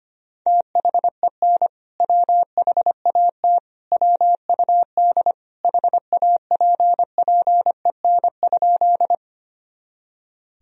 Morse code, heard as a keyed tone, about 25 wpm, 700 Hz. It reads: THEN WHAT WUD HAPPEN?
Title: Sentences from Top 400 Words and Abbreviations 25wpm